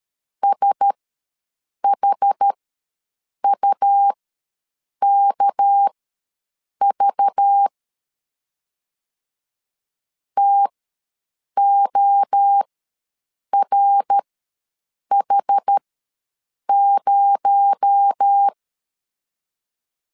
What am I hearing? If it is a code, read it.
Morse code, heard as a keyed tone, about 13 wpm, spaced as for 7 wpm, 800 Hz. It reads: SHUKV TORH0
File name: LEZIONI / 6 PKX0 / 6 PKX0 ws 12wpm 3448